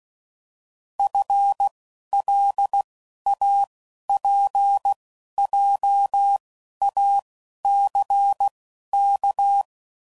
Morse code, heard as a keyed tone, about 16 wpm, spaced as for 11 wpm, 800 Hz. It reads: FLAPJACK
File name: Prefixes / FLs & THs / FL Runs